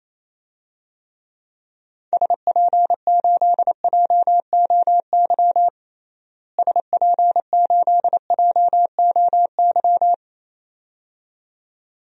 Morse code, heard as a keyed tone, about 28 wpm, 700 Hz. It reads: SP8JOY SP8JOY
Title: DXCC-100 Call Signs - 6 Characters - Repeated 2x - 28wpm